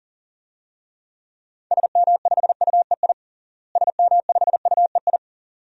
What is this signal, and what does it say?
Morse code, heard as a keyed tone, about 40 wpm, 700 Hz. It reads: SM5UEI SM5UEI
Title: DXCC-100 Call Signs - 6 Characters - Repeated 2x - 40wpm